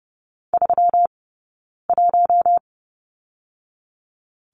Morse code, heard as a keyed tone, about 30 wpm, 700 Hz. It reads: 3 1
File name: Lesson 20d - Number 3 - Two Characters 30wpm